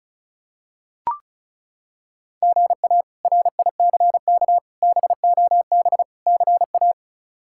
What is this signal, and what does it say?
Morse code, heard as a keyed tone, about 36 wpm, 700 Hz. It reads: GA RICK BOB CA
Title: K1USN SST QSO Practice - 35wpm